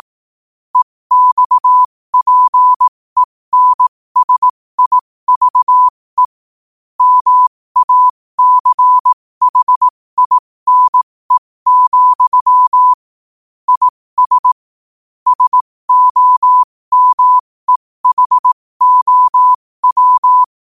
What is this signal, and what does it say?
Morse code, heard as a keyed tone, about 18 wpm, spaced as for 15 wpm, 1000 Hz. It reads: EXPENSIVE MACHINE, IS SOMEHOW
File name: qotdfiles / 15wpm